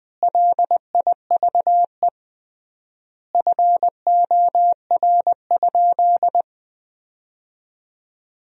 Morse code, heard as a keyed tone, about 19 wpm, 700 Hz. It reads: LIVE FOR?
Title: 3x Word-Spacing - Sentences from Top 200 Words - 20wpm